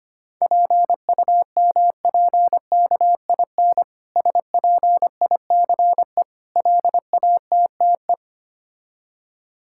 Morse code, heard as a keyed tone, about 25 wpm, 690 Hz. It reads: PUMPKIN SPICE LATTE
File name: practice / 25wpm / Difficult Words with Hints - 25wpm